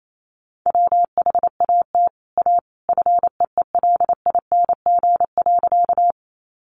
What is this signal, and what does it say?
Morse code, heard as a keyed tone, about 28 wpm, 700 Hz. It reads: WHAT A FEELING.
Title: Sentences from Top 1200 Words 28wpm